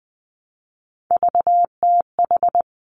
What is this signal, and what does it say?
Morse code, heard as a keyed tone, about 20 wpm, 700 Hz. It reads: VTH
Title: Three Challenging Letter-Number Combinations 20wpm